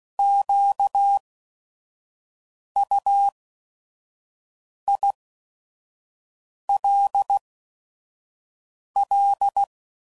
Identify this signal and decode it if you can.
Morse code, heard as a keyed tone, about 16 wpm, spaced as for 5 wpm, 800 Hz. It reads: QUILL
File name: Prefixes / By First Letter / Qs